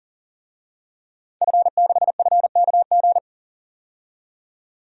Morse code, heard as a keyed tone, about 40 wpm, 700 Hz. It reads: F6FKG
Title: DXCC-100 Call Signs - 5 Characters - 40wpm